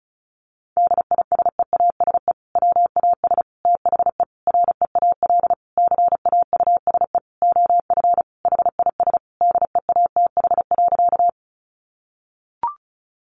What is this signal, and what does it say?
Morse code, heard as a keyed tone, about 34 wpm, 700 Hz. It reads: DISEASE WAS THE REAL CAUSE OF HIS DEATH.
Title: Sentences from Top 1100 Words 35wpm